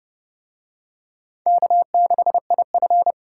Morse code, heard as a keyed tone, about 29 wpm, 700 Hz. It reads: K6IF